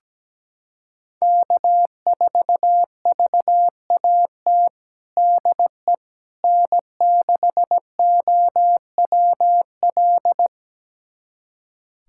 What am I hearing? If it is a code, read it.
Morse code, heard as a keyed tone, about 17 wpm, 700 Hz. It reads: K4VAT DE N6OWL
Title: USA Callsign DE Callsign - 5 Characters - 17wpm